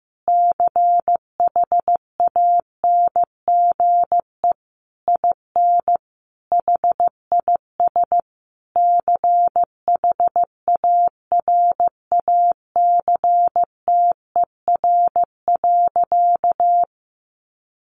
Morse code, heard as a keyed tone, about 15 wpm, 700 Hz. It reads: CHANGE IN HIS CHARACTER.
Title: Sentences from Top 1200 Words 15wpm